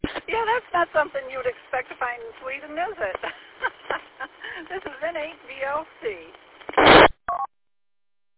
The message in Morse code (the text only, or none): none